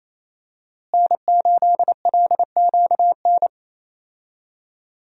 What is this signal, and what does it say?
Morse code, heard as a keyed tone, about 28 wpm, 700 Hz. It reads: N8LQN
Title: USA Call Signs - 5 Characters - Repeated 2x - 28wpm